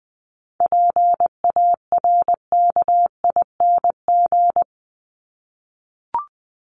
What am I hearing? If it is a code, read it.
Morse code, heard as a keyed tone, about 20 wpm, 700 Hz. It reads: PARKING
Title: Top 1501-1750 Words 20wpm